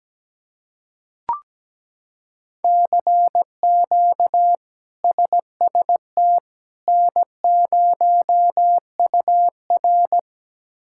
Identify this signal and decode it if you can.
Morse code, heard as a keyed tone, about 17 wpm, 700 Hz. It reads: CQ SST N0UR
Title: K1USN SST QSO Practice - 17wpm